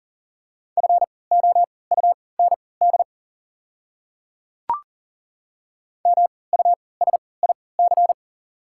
Morse code, heard as a keyed tone, about 40 wpm, spaced as for 23 wpm, 690 Hz. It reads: FOUND MUSIC
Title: 3x Character Spacing - Top 300 Words - 40wpm